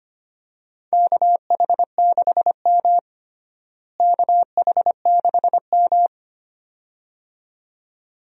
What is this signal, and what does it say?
Morse code, heard as a keyed tone, about 25 wpm, 700 Hz. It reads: KH6M KH6M
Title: USA Call Signs - 4 Characters - Repeated 2x - 25wpm